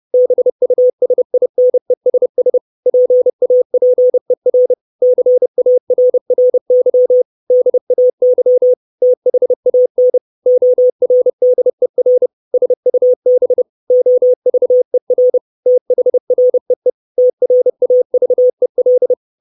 Morse code, heard as a keyed tone, about 30 wpm, 500 Hz. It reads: BUSINESS PAPER CARRY DAY THAN ORDER SUB OVER THREE TRAVEL
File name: 632 Words 30WPM